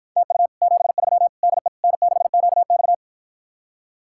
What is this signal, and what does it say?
Morse code, heard as a keyed tone, about 53 wpm, 700 Hz. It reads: TU 73 DE N6QX